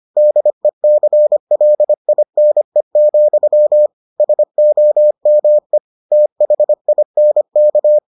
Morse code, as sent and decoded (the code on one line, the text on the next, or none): -.. -.-. .-.. .. -. --..-- / ... --- -- . / - .... .. -. -.-
DECLINE, SOME THINK